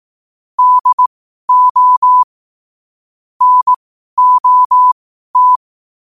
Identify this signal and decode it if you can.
Morse code, heard as a keyed tone, about 18 wpm, spaced as for 12 wpm, 1000 Hz. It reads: DO NOT
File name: qotdfiles / 12wpm